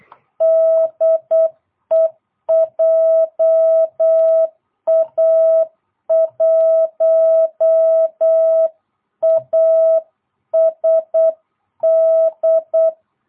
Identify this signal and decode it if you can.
Morse code, heard as a keyed tone, about 8 wpm, 660 Hz. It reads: DE JA1ASD